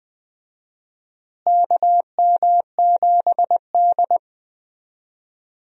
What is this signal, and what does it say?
Morse code, heard as a keyed tone, about 20 wpm, 700 Hz. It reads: KM7D
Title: USA Call Signs - 4 Characters - Repeated 2x - 20wpm